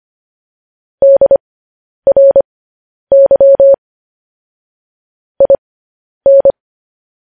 Morse code, heard as a keyed tone, about 25 wpm, spaced as for 10 wpm, 550 Hz. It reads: DRY IN
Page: news-headlines-in-morse-code-at-10-wpm